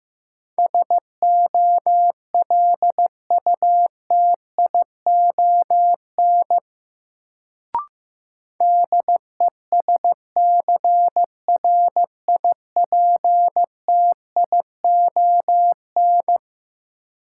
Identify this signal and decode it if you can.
Morse code, heard as a keyed tone, about 15 wpm, 700 Hz. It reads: SOLUTION DESCRIPTION
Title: Sentences from Top 1300 Words - New Word Drill Alt - 15wpm